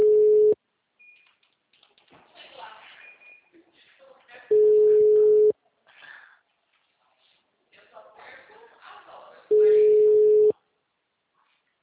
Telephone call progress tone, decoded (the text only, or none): ringback tone